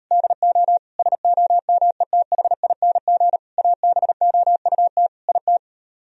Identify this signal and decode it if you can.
Morse code, heard as a keyed tone, about 38 wpm, 700 Hz. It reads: DO SOMETHING ABOUT IT